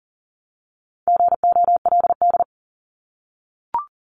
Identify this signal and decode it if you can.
Morse code, heard as a keyed tone, about 42 wpm, 700 Hz. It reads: GOLD
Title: Multi-letter Phonemes Repeated 3x 40wpm